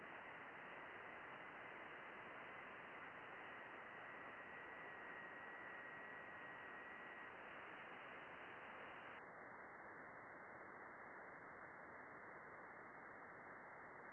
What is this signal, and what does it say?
FT8.
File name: contests / june-vhf-k5tr- -0200